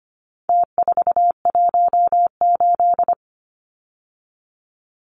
Morse code, heard as a keyed tone, about 25 wpm, 700 Hz. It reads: T418
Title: Numbers - Contest Style - 4-Digit 25wpm